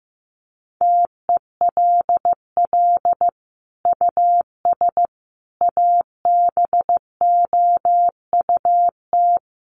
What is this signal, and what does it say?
Morse code, heard as a keyed tone, about 15 wpm, 700 Hz. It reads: TELL US ABOUT
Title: Sentences from Top 1500 Words 15wpm